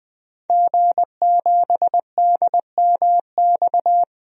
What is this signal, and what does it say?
Morse code, heard as a keyed tone, about 20 wpm, 700 Hz. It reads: G7DMX
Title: DXCC-100 Call Signs - 5 Characters - 20wpm